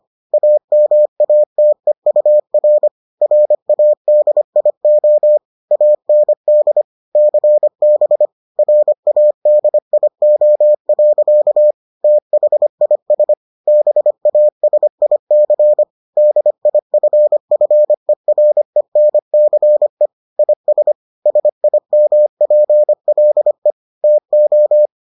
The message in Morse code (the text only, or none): AMATEUR RADIO AND CB RADIO. THIS BASIC DIFFERENCE IS SIMPLE TO